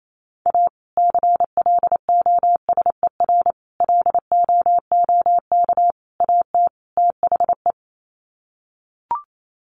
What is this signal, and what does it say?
Morse code, heard as a keyed tone, about 28 wpm, 700 Hz. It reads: A CLOSER LOOK AT THE
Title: Sets of 5 Words Encore 28wpm